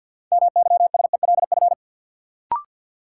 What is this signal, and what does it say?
Morse code, heard as a keyed tone, about 52 wpm, 700 Hz. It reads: MYSELF
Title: Sentences from Top 1300 Words - New Word Drill - 50wpm